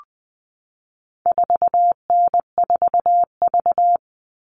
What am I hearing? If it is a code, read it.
Morse code, heard as a keyed tone, about 20 wpm, 700 Hz. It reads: 4N4V
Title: All Cut Numbers - 4-Digit 20wpm